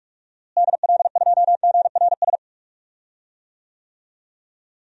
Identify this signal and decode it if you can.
Morse code, heard as a keyed tone, about 41 wpm, 700 Hz. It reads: DL2GRS
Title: DXCC-100 Call Signs - 6 Characters - Repeated 3x - 45wpm